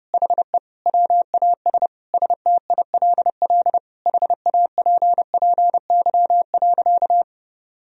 Morse code, heard as a keyed tone, about 32 wpm, 700 Hz. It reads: HE WAS STILL HAPPY.
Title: Sentences from Top 1000 Words - Review Part 3 - 30wpm